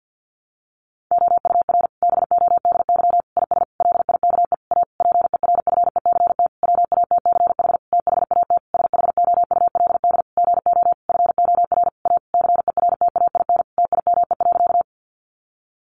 Morse code, heard as a keyed tone, about 50 wpm, 700 Hz. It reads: OUR BODY IS LIKE A PERFECT WATCH THAT SHOULD GO FOR A CERTAIN TIME.
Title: Sentences from Top 1000 Words 50wpm